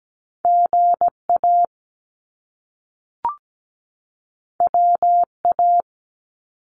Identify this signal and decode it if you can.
Morse code, heard as a keyed tone, about 17 wpm, 700 Hz. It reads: GA WA
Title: US States 17wpm